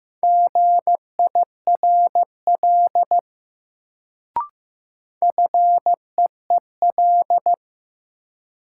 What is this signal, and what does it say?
Morse code, heard as a keyed tone, about 15 wpm, 700 Hz. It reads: GIRL FEEL
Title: Sentences from Top 400 Words - New Word Drill - 15wpm